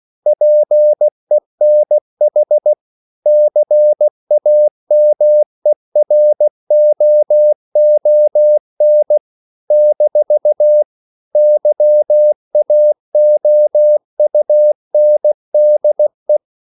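Morse code, heard as a keyed tone, about 16 wpm, 600 Hz. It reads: PENH CAMEROON - YAOUNDE
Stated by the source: CW practice of national capitals at 16 words per minute, from ON6ZQ.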